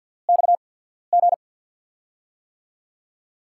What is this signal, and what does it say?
Morse code, heard as a keyed tone, about 49 wpm, spaced as for 14 wpm, 700 Hz. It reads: XG
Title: USA Call Signs - 5 Characters - 8x Character Spacing - 50wpm